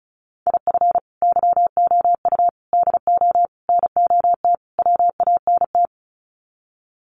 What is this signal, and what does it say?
Morse code, heard as a keyed tone, about 34 wpm, 700 Hz. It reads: IF YOU DO NOT WANT
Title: Sets of 5 Words 35wpm